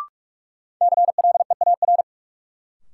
Morse code, heard as a keyed tone, about 46 wpm, 700 Hz. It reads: CLEAR
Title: Sets of 5 Words Encore - New Word Drill - 45wpm